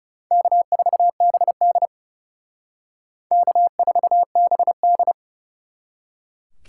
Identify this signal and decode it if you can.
Morse code, heard as a keyed tone, about 38 wpm, 700 Hz. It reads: K4BD K4BD